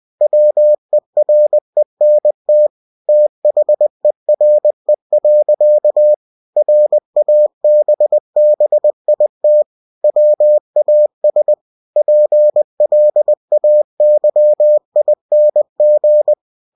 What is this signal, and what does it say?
Morse code, heard as a keyed tone, about 20 wpm, 600 Hz. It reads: WERENT THERE. RABBIT WAS PLAYING
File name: cw-books / winnie-the-pooh / winnie-20wpm-page-0071